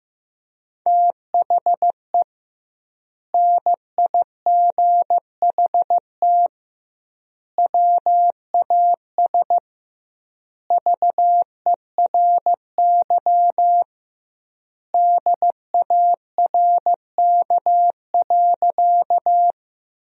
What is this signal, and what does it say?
Morse code, heard as a keyed tone, about 15 wpm, 700 Hz. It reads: THE NIGHT WAS VERY DARK.
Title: 2x Word-Spacing - Sentences from Top 500 Words - 15wpm